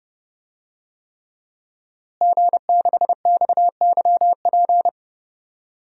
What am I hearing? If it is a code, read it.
Morse code, heard as a keyed tone, about 32 wpm, 700 Hz. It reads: G6XYP
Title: DXCC-100 Call Signs - 5 Characters - Repeated 3x - 30wpm